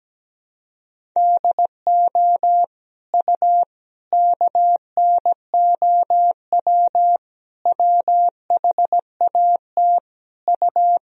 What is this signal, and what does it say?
Morse code, heard as a keyed tone, about 17 wpm, 700 Hz. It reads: DO U KNOW WHAT U
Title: Sentences from Top 100 Words and Abbreviations 17wpm